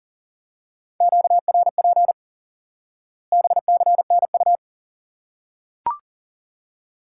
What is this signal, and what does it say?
Morse code, heard as a keyed tone, about 40 wpm, 700 Hz. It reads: QRP BCNU
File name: IQE Course 3 - Lesson 89 - YAGI - Two Elements - 40wpm